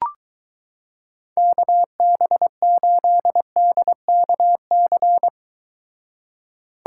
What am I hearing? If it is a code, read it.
Morse code, heard as a keyed tone, about 23 wpm, 700 Hz. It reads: KB8DKC